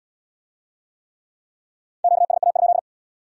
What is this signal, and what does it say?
Morse code, heard as a keyed tone, about 77 wpm, 700 Hz. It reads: QSA?